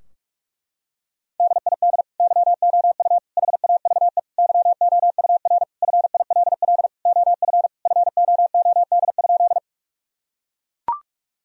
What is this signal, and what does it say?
Morse code, heard as a keyed tone, about 41 wpm, 700 Hz. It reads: DID YOU HAVE YOUR FILL OF FOOD?
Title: Sentences from Top 1000 Words - Review Part 3 - 45wpm